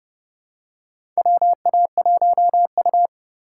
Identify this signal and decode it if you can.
Morse code, heard as a keyed tone, about 30 wpm, 700 Hz. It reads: WA1U
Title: USA Call Signs - 4 Characters - 4x Character Spacing - 30wpm